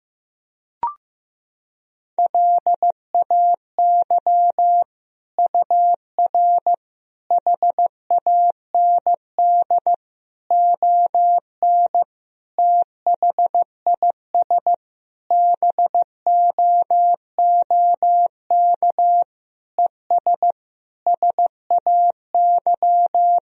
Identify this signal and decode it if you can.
Morse code, heard as a keyed tone, about 16 wpm, 700 Hz. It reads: LAY UR HAND ON THIS BOOK ES SAY
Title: Sentences from Top 400 Words and Abbreviations 15wpm